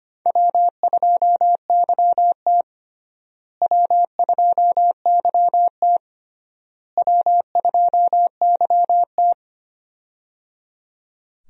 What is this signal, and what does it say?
Morse code, heard as a keyed tone, about 25 wpm, 700 Hz. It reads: W2YT W2YT W2YT
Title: USA Call Signs - 4 Characters - Repeated 3x - 25wpm